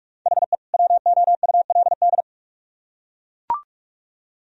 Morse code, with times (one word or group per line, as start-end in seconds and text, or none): 0.3-0.6 HE
0.7-2.2 WOULD